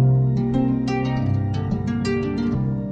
There is no code.